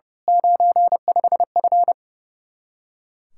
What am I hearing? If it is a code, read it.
Morse code, heard as a keyed tone, about 28 wpm, 700 Hz. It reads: 95F
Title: Speed-Racing Common QSO Phrases Encore - 30wpm